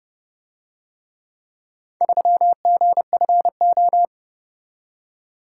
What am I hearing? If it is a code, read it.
Morse code, heard as a keyed tone, about 29 wpm, 700 Hz. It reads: 3GFO